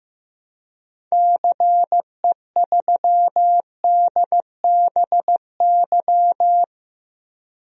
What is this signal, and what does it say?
Morse code, heard as a keyed tone, about 15 wpm, 700 Hz. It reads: CE3DBY
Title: DXCC-100 Call Signs - 6 Characters - 15wpm